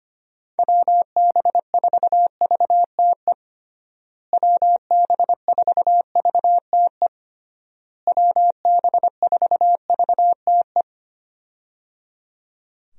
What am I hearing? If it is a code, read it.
Morse code, heard as a keyed tone, about 25 wpm, 700 Hz. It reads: WB4VTE WB4VTE WB4VTE
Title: USA Call Signs - 6 Characters - Repeated 3x - 25wpm